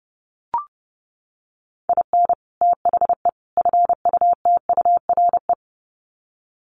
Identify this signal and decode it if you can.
Morse code, heard as a keyed tone, about 30 wpm, 700 Hz. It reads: IN THE FUTURE
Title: Sets of 3 Words 30wpm